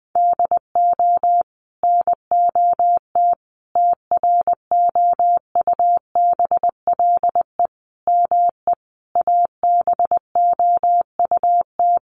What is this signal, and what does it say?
Morse code, heard as a keyed tone, about 20 wpm, 700 Hz. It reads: DO NOT TROUBLE ME ABOUT
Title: Sentences from Top 1200 Words 20wpm